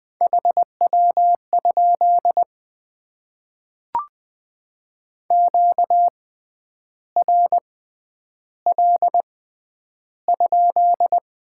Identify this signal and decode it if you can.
Morse code, heard as a keyed tone, about 20 wpm, spaced as for 9 wpm, 700 Hz. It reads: HW? QRL?
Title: Top 90 QSO Elements - 6x Character Spacing - 20wpm